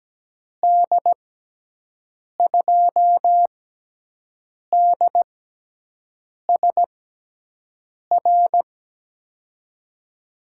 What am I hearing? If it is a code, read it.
Morse code, heard as a keyed tone, about 17 wpm, spaced as for 6 wpm, 700 Hz. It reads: D2DSR
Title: USA Call Signs - 6x Character Spacing - 17wpm